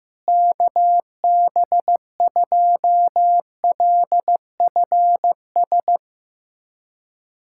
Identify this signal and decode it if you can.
Morse code, heard as a keyed tone, about 16 wpm, 700 Hz. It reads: KB2LFS
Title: USA Callsign DE Callsign - 6 Characters - 15wpm